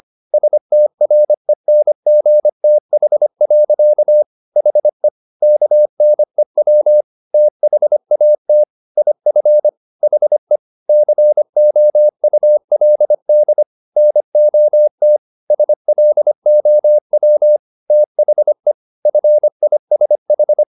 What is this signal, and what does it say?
Morse code, heard as a keyed tone, about 25 wpm, 600 Hz. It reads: STRENGTH. HE KNEW THAT IF HE COULD NOT SLOW THE FISH
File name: the old man and the sea part 1 25wpm